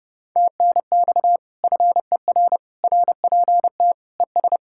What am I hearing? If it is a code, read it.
Morse code, heard as a keyed tone, about 36 wpm, 700 Hz. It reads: TNX FER RPT ES
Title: Speed-Racing Common QSO Phrases - 30wpm